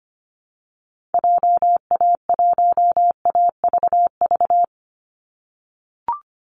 Morse code, heard as a keyed tone, about 25 wpm, 700 Hz. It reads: JA1AVV
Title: USA Call Signs 25wpm